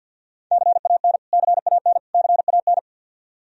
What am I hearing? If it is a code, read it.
Morse code, heard as a keyed tone, about 47 wpm, 700 Hz. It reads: CAN CAN CAN
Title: Top 100 Words Repeated 50wpm